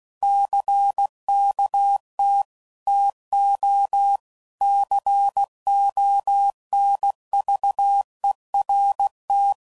Morse code, heard as a keyed tone, about 16 wpm, 800 Hz. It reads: CKT TO CONVERT
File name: CW Code Groups @15WPM/Technical Subjects